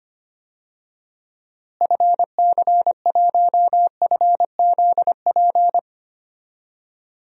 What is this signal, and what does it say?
Morse code, heard as a keyed tone, about 25 wpm, 700 Hz. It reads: FC1FZP
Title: DXCC-100 Call Signs - 6 Characters - Repeated 3x - 25wpm